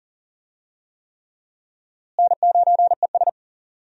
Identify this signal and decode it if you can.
Morse code, heard as a keyed tone, about 37 wpm, 700 Hz. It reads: N9ES
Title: USA Call Signs - 4 Characters - Repeated 3x - 40wpm